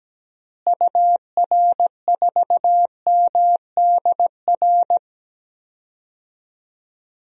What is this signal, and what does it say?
Morse code, heard as a keyed tone, about 17 wpm, 700 Hz. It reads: UR4MDR
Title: DXCC-100 Call Signs - 6 Characters - Repeated 3x - 17wpm